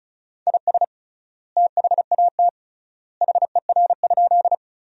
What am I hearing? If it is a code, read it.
Morse code, heard as a keyed tone, about 36 wpm, 700 Hz. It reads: IS THAT HER?
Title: 3x Word-Spacing - Sentences from Top 100 Words - 35wpm